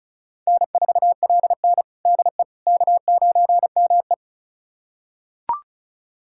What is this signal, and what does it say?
Morse code, heard as a keyed tone, about 36 wpm, 700 Hz. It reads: N4LN DE K9ME
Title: USA Callsign DE Callsign - 4 Characters - 35wpm